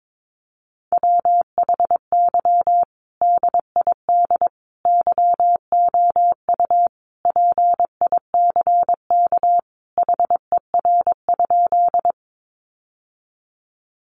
Morse code, heard as a keyed tone, about 22 wpm, 700 Hz. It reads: WHY DID YOU PICK HER?
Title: Sentences from Top 1000 Words 22wpm